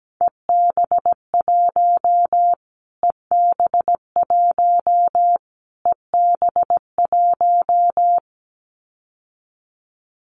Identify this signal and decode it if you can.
Morse code, heard as a keyed tone, about 17 wpm, 700 Hz. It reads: EB1 EB1 EB1
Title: Callsign Prefixes for Top 25 DXCC - Repeated 3x - 17wpm